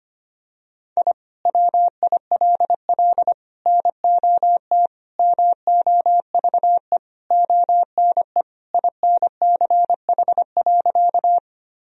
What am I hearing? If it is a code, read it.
Morse code, heard as a keyed tone, about 25 wpm, 700 Hz. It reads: I WILL NOT MOVE ONE INCH.
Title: Sentences from Top 1000 Words - Review Part 3 - 25wpm